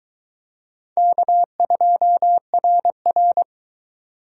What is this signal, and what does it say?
Morse code, heard as a keyed tone, about 22 wpm, 700 Hz. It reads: K2RR